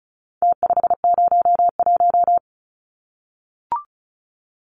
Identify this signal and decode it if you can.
Morse code, heard as a keyed tone, about 34 wpm, 700 Hz. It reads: T501
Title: Numbers - Contest Style - 4-Digit 35wpm